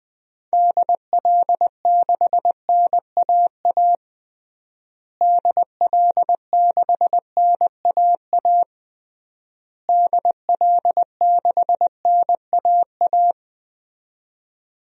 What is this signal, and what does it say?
Morse code, heard as a keyed tone, about 21 wpm, 700 Hz. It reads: DL6NAA DL6NAA DL6NAA
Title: DXCC-100 Call Signs - 6 Characters - Repeated 3x - 20wpm